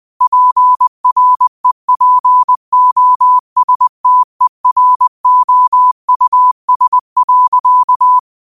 Morse code, heard as a keyed tone, about 20 wpm, 1000 Hz. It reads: PREPOSTEROUS.
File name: 20wpm